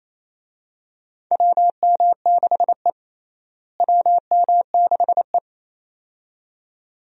Morse code, heard as a keyed tone, about 28 wpm, 700 Hz. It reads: WM6E WM6E
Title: USA Call Signs - 4 Characters - Repeated 2x - 28wpm